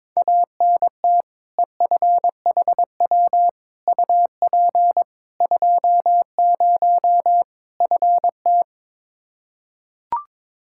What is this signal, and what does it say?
Morse code, heard as a keyed tone, about 22 wpm, 700 Hz. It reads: ANT EFHW UP 20 FT
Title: Practice QSO - Exchange 2 - Info - 22wpm